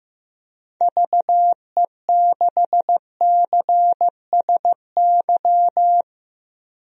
Morse code, heard as a keyed tone, about 15 wpm, 700 Hz. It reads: VE6CSY
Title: DXCC-100 Call Signs - 6 Characters - 15wpm